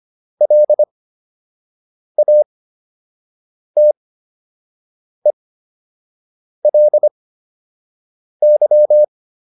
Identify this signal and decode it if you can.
Morse code, heard as a keyed tone, about 25 wpm, spaced as for 6 wpm, 600 Hz. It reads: LATELY